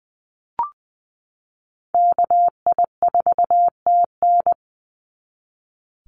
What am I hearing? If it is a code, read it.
Morse code, heard as a keyed tone, about 20 wpm, 700 Hz. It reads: KI4TN
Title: NA SOTA Chaser Callsigns 20wpm